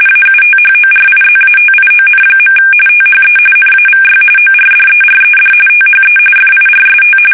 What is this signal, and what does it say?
RTTY, about 45.45 baud. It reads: AUTHORIZATION. THE NAVY ORDERED THE FIRST
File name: fsk850 / battleshipnj